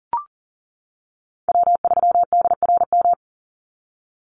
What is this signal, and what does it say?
Morse code, heard as a keyed tone, about 40 wpm, 700 Hz. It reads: W3DRM
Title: USA Call Signs 40wpm